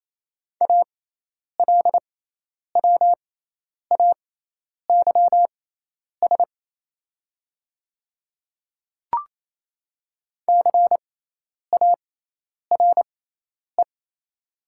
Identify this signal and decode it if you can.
Morse code, heard as a keyed tone, about 28 wpm, spaced as for 10 wpm, 690 Hz. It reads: ALWAYS CARE
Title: 6x Character Spacing - Top 300 Words - 28wpm